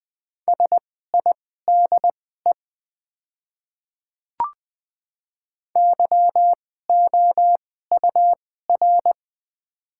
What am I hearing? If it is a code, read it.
Morse code, heard as a keyed tone, about 20 wpm, spaced as for 14 wpm, 700 Hz. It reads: SIDE YOUR
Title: 2x Character Spacing - Top 200 Words - 20wpm